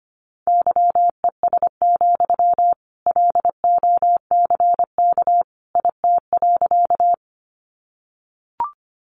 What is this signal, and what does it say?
Morse code, heard as a keyed tone, about 25 wpm, 700 Hz. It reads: YES, LOCK IT.
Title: Sentences from Top 1300 Words 25wpm